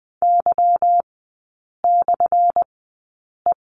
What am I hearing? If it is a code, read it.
Morse code, heard as a keyed tone, about 20 wpm, spaced as for 8 wpm, 700 Hz. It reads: Y/E
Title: 218 Lesson 35e - Number 0 - Three Characters 20wpm